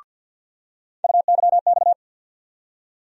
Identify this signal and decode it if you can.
Morse code, heard as a keyed tone, about 49 wpm, 700 Hz. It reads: UYX
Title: USA Call Sign Postfixes - 3 Letters 50wpm